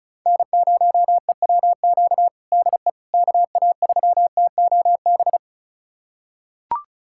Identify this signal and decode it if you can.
Morse code, heard as a keyed tone, about 36 wpm, 700 Hz. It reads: N0EWQ DE KA3TOB
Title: USA Callsign DE Callsign 35wpm